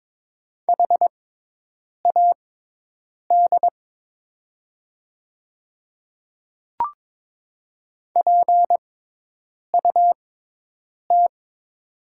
Morse code, heard as a keyed tone, about 22 wpm, spaced as for 8 wpm, 700 Hz. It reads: HAD PUT
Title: 6x Character Spacing - Top 300 Words - 22wpm